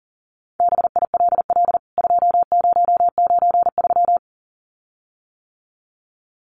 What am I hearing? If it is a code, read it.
Morse code, heard as a keyed tone, about 40 wpm, 700 Hz. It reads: BILL 2093
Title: CWT Exchange 40wpm